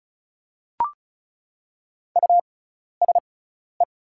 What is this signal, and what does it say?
Morse code, heard as a keyed tone, about 34 wpm, spaced as for 12 wpm, 690 Hz. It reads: USE
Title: 6x Character Spacing - Top 500 Words - 35wpm